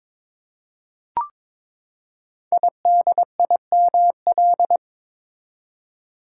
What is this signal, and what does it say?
Morse code, heard as a keyed tone, about 22 wpm, 700 Hz. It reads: IDIML
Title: Code Groups - 5 Letter:Numbers - 22wpm